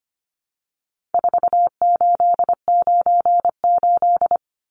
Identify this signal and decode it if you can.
Morse code, heard as a keyed tone, about 25 wpm, 700 Hz. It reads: 4898
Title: Numbers - 4-Digit 25wpm